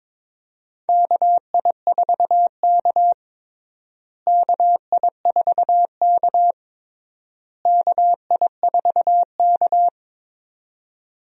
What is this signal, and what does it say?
Morse code, heard as a keyed tone, about 22 wpm, 700 Hz. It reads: KI4K KI4K KI4K
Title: USA Call Signs - 4 Characters - Repeated 3x - 22wpm